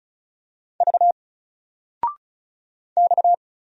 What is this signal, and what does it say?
Morse code, heard as a keyed tone, about 35 wpm, 700 Hz. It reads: V X
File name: Single-Letter-Number-35wpm